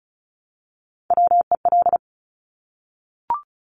Morse code, heard as a keyed tone, about 36 wpm, 700 Hz. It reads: WEL
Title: Top Three Letter Combinations in Words 35wpm